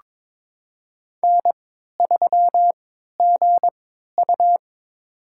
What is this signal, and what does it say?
Morse code, heard as a keyed tone, about 22 wpm, spaced as for 12 wpm, 700 Hz. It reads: N3GU
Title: USA Call Signs - 4 Characters - 3x Character Spacing - 22wpm